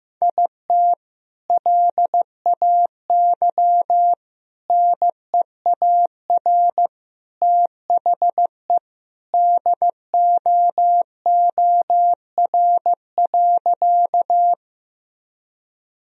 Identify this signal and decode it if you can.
Morse code, heard as a keyed tone, about 15 wpm, 700 Hz. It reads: IT LAY NEAR THE DOOR.
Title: Sentences from Top 1000 Words - Review Part 2 - 15wpm